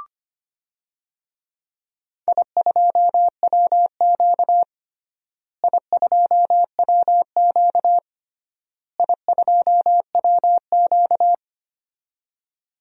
Morse code, heard as a keyed tone, about 25 wpm, 700 Hz. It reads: I2WQ I2WQ I2WQ
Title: DXCC-100 Call Signs - 4 Characters - Repeated 3x - 25wpm